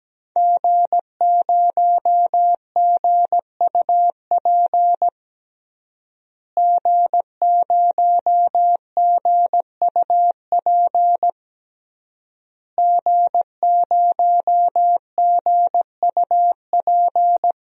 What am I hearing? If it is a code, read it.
Morse code, heard as a keyed tone, about 17 wpm, 700 Hz. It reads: G0GUP G0GUP G0GUP
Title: DXCC-100 Call Signs - 5 Characters - Repeated 3x - 17wpm